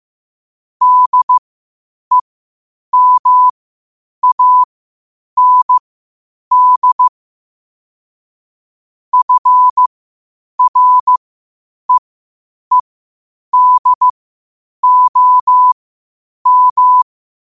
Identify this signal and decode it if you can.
Morse code, heard as a keyed tone, about 15 wpm, spaced as for 8 wpm, 1000 Hz. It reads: DEMAND FREEDOM
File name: qotdfiles / 8wpm